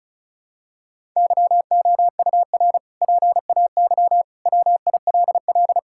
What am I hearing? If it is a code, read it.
Morse code, heard as a keyed tone, about 36 wpm, 700 Hz. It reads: YOUR PAY WILL
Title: Sentences from Top 1000 Words - Review Part 6 - 35wpm